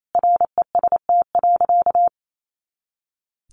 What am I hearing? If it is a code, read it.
Morse code, heard as a keyed tone, about 28 wpm, 700 Hz. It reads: REST.
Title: Sentences from Top 500 Words 28wpm